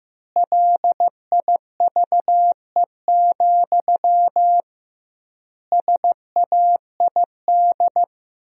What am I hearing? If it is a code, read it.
Morse code, heard as a keyed tone, about 15 wpm, 700 Hz. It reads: LIVE, SAID
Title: 2x Word-Spacing - Sentences from Top 200 Words - 15wpm